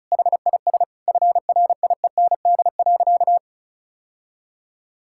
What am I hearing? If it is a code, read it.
Morse code, heard as a keyed tone, about 36 wpm, 700 Hz. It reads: HIS FRIEND.
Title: Sentences from Top 1600 Words 35wpm